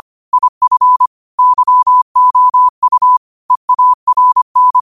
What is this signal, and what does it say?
Morse code, heard as a keyed tone, about 25 wpm, 1000 Hz. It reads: IF YOU EARN
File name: 25wpm